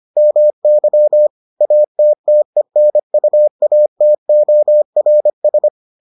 Morse code, heard as a keyed tone, about 25 wpm, 600 Hz. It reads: MY ATTENUATORS